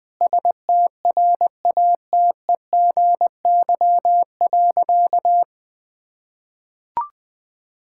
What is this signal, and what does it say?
Morse code, heard as a keyed tone, about 20 wpm, 700 Hz. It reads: STRATEGY.
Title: Sentences from Top 1000 Words - Review Part 5 - 20wpm